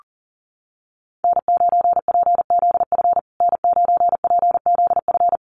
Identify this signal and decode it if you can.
Morse code, heard as a keyed tone, about 37 wpm, 700 Hz. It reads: N9PZF N9PZF
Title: USA Call Signs Repeated 3x 40wpm